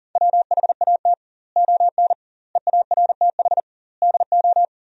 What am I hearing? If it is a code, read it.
Morse code, heard as a keyed tone, about 40 wpm, 700 Hz. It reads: WHAT ON EARTH DO
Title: 2x Word-Spacing - Sentences from Top 200 Words - 40wpm